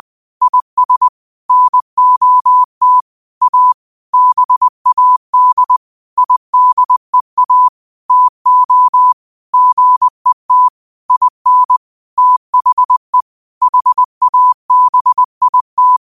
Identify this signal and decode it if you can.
Morse code, heard as a keyed tone, about 20 wpm, 1000 Hz. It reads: IS NOT A BAD IDEA TO GET IN THE HABIT